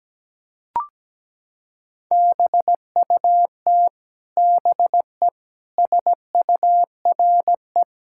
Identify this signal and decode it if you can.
Morse code, heard as a keyed tone, about 17 wpm, 700 Hz. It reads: BUT BE SURE
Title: Sentences from Top 1600 Words 17wpm